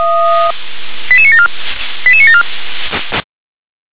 Five-tone selective call (ZVEI), heard as quick runs of five tones, tones of about 70 ms. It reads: 88974 88974
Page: /357/SDR/ZVEI/